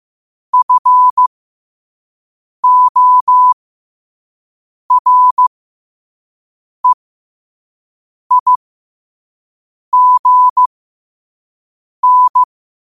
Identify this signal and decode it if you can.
Morse code, heard as a keyed tone, about 15 wpm, spaced as for 5 wpm, 1000 Hz. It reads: FOREIGN